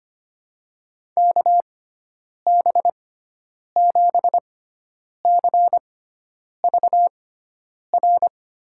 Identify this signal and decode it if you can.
Morse code, heard as a keyed tone, about 25 wpm, spaced as for 9 wpm, 700 Hz. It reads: KB7CVR
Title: USA Call Signs - 6 Characters - 6x Character Spacing - 25wpm